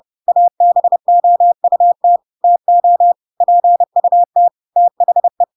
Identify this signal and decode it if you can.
Morse code, heard as a keyed tone, about 30 wpm, 700 Hz. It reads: ABOUT TO PUT THE
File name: Pinocchio / Pinocchio ch3 30 30 ik1hge